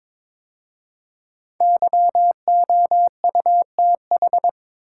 Morse code, heard as a keyed tone, about 21 wpm, 700 Hz. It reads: YOUTH
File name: 5 Letter Words - First 3 Letters Given - 22wpm